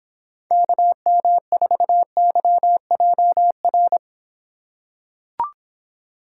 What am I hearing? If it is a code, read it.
Morse code, heard as a keyed tone, about 26 wpm, 700 Hz. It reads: KM4YJR